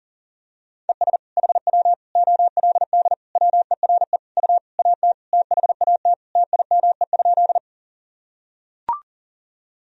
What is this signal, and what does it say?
Morse code, heard as a keyed tone, about 40 wpm, 700 Hz. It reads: ES HW OLD WERE U AT THAT TIME?
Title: Sentences from Top 200 Words and Abbreviations 40wpm